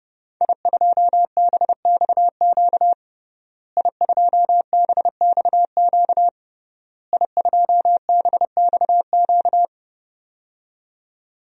Morse code, heard as a keyed tone, about 30 wpm, 700 Hz. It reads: I2BXQ I2BXQ I2BXQ
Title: DXCC-100 Call Signs - 5 Characters - Repeated 3x - 30wpm